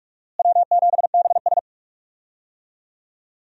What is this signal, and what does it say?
Morse code, heard as a keyed tone, about 43 wpm, 700 Hz. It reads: W7BS
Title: USA Call Signs - 4 Characters - Repeated 2x - 45wpm